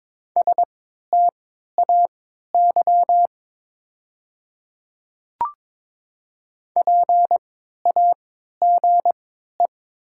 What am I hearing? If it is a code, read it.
Morse code, heard as a keyed tone, about 22 wpm, spaced as for 13 wpm, 700 Hz. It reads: STAY PAGE
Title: 3x Character Spacing - Top 500 Words - 22wpm